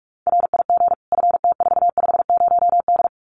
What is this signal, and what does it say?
Morse code, heard as a keyed tone, about 44 wpm, 700 Hz. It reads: RIG FT450D
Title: Common QSO Phrases 45wpm